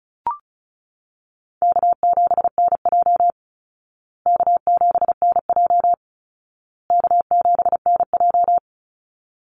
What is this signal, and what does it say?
Morse code, heard as a keyed tone, about 34 wpm, 700 Hz. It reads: K7NJ K7NJ K7NJ
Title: Easy Call Signs Repeated 3x - 35wpm